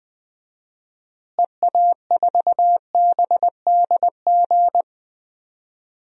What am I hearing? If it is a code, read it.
Morse code, heard as a keyed tone, about 20 wpm, 700 Hz. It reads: EA4BDG
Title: DXCC-100 Call Signs - 6 Characters - Repeated 3x - 20wpm